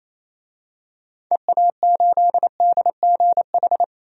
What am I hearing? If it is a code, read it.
Morse code, heard as a keyed tone, about 28 wpm, 700 Hz. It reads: EA8DGH
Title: DXCC-100 Call Signs - 6 Characters - 28wpm